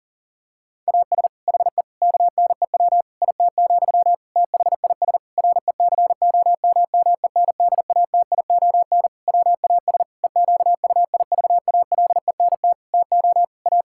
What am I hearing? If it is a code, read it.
Morse code, heard as a keyed tone, about 40 wpm, 700 Hz. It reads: AS HE KNEW IT, THIS RECOMMENDATION WAS EQUIVALENT TO A
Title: Sentences from Top 1600 Words 40wpm